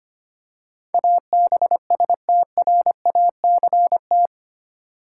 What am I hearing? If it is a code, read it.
Morse code, heard as a keyed tone, about 25 wpm, 700 Hz. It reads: ABSTRACT
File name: Eight Letter Words - 25wpm